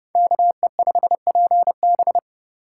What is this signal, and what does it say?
Morse code, heard as a keyed tone, about 31 wpm, 700 Hz. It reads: KE5PB